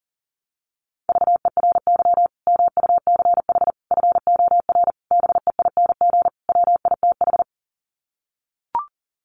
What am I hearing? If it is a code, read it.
Morse code, heard as a keyed tone, about 40 wpm, 700 Hz. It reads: VERY MUCH FOR BEING WITH
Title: Sets of 5 Words 40wpm